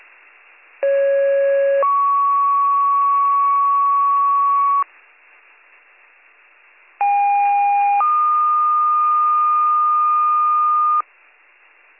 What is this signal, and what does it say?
Two-tone page, heard as a one-second tone, then a three-second tone, about 550 then 1120 Hz, about 810 then 1220 Hz. It is heard from 0.8 to 11.0 s.